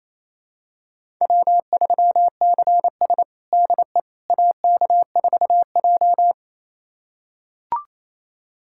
Morse code, heard as a keyed tone, about 28 wpm, 700 Hz. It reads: W3CS DE AK4J
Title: USA Callsign DE Callsign - 4 Characters - 28wpm